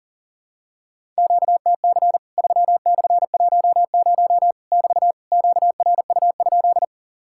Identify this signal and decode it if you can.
Morse code, heard as a keyed tone, about 38 wpm, 700 Hz. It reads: QTC 3/10 <BT> QRU?
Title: Worked All Europe DX Contest - QTC Exchange - 40wpm